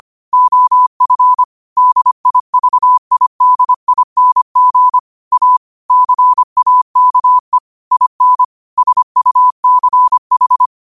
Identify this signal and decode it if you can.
Morse code, heard as a keyed tone, about 25 wpm, 1000 Hz. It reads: OF DIVIDING A CAKE IN SUCH
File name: qotdfiles / 25wpm